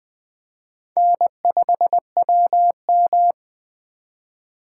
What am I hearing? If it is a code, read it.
Morse code, heard as a keyed tone, about 20 wpm, 700 Hz. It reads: N5WM
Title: Speed Racing - USA 1x2 Call Signs 17wpm